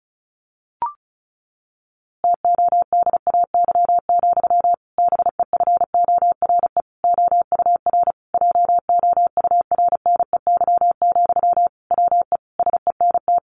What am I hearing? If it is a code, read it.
Morse code, heard as a keyed tone, about 34 wpm, 700 Hz. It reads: TODAY, BEFORE OUR JOURNEY, WE SENT
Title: A Sound Of Thunder 35wpm